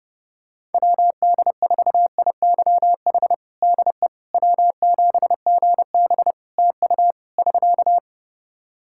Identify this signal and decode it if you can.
Morse code, heard as a keyed tone, about 32 wpm, 700 Hz. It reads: WD4IYH DE W7GB TU <SK>
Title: Practice QSO - Exchange 4 - Ending - 30wpm